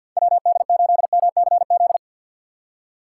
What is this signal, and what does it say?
Morse code, heard as a keyed tone, about 47 wpm, 700 Hz. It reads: WD8MCZ
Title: USA Call Signs - 6 Characters - 4x Character Spacing - 50wpm